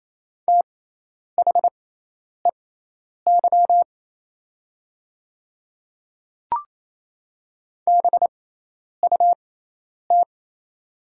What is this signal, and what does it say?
Morse code, heard as a keyed tone, about 28 wpm, spaced as for 10 wpm, 700 Hz. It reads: THEY BUT
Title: 6x Character Spacing - Top 100 Words - 28wpm